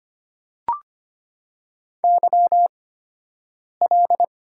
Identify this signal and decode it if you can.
Morse code, heard as a keyed tone, about 25 wpm, spaced as for 7 wpm, 700 Hz. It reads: YL
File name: Top 90 QSO Elements - 8x Character Spacing - 25wpm